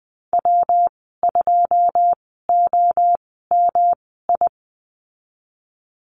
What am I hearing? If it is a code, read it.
Morse code, heard as a keyed tone, about 20 wpm, spaced as for 15 wpm, 700 Hz. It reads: W2OMI
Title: 186 Lesson 30g - Letter b - Call Signs 20wpm